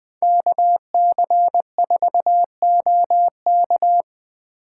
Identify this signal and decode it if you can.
Morse code, heard as a keyed tone, about 20 wpm, 700 Hz. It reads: KC4OK
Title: USA Call Signs - 5 Characters - 3x Character Spacing - 20wpm